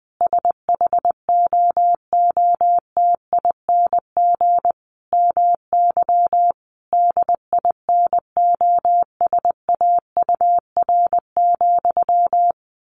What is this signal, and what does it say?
Morse code, heard as a keyed tone, about 20 wpm, 700 Hz. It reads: SHOOTING MY DINOSAUR,
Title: A Sound Of Thunder 20wpm